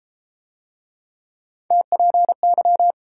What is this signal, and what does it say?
Morse code, heard as a keyed tone, about 34 wpm, 700 Hz. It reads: TPY